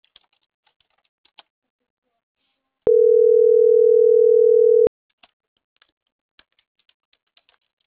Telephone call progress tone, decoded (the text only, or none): ringback tone